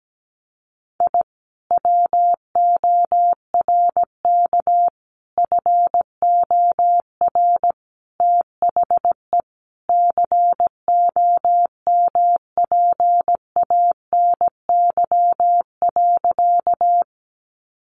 Morse code, heard as a keyed tone, about 17 wpm, 700 Hz. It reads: I WORK FOR THE COMPANY.
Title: Sentences from Top 600 Words 17wpm